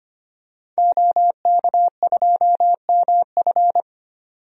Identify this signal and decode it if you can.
Morse code, heard as a keyed tone, about 25 wpm, 690 Hz. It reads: OK2MF
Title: DXCC-100 Call Signs - 5 Characters - 25wpm